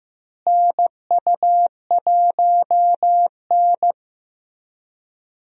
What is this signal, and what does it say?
Morse code, heard as a keyed tone, about 15 wpm, 700 Hz. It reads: NU1N
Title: USA Call Signs - 4 Characters - Repeated 3x - 15wpm